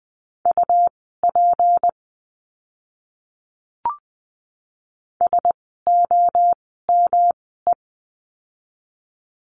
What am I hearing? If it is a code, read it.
Morse code, heard as a keyed tone, about 20 wpm, spaced as for 14 wpm, 700 Hz. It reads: UP SOME
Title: 206 Lesson 33f - Letter j - Words 20wpm